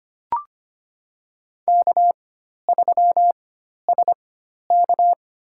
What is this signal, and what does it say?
Morse code, heard as a keyed tone, about 25 wpm, spaced as for 12 wpm, 700 Hz. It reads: K3SK
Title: USA Call Signs - 4 Characters - 4x Character Spacing - 25wpm